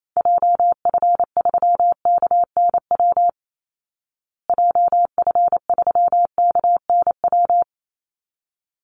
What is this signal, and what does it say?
Morse code, heard as a keyed tone, about 28 wpm, 700 Hz. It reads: JF3KNW JF3KNW
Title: Call Signs Repeated 3x - 28wpm